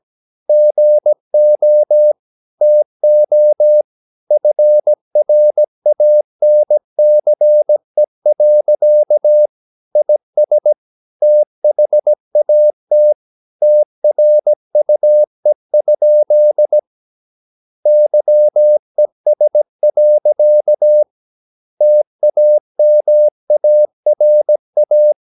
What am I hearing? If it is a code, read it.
Morse code, heard as a keyed tone, about 17 wpm, 600 Hz. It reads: GO TO FRANCE. IS THAT TRUE? YES. TAMARA
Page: Romanen Never af Ken Follet på engelsk.